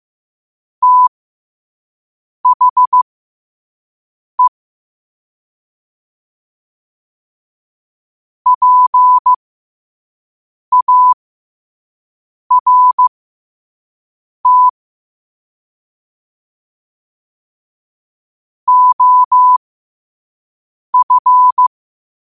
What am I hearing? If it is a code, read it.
Morse code, heard as a keyed tone, about 15 wpm, spaced as for 5 wpm, 1000 Hz. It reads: THE PART OF
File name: qotdfiles / 5wpm